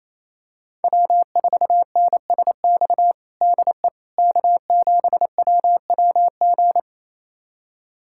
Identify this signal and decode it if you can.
Morse code, heard as a keyed tone, about 28 wpm, 700 Hz. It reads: W4NSX DE K7WWG
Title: USA Callsign DE Callsign 28wpm